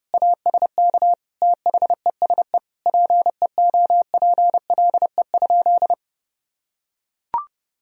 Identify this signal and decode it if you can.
Morse code, heard as a keyed tone, about 30 wpm, 700 Hz. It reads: ASK THESE PEOPLE?
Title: Sentences from Top 200 Words and Abbreviations 30wpm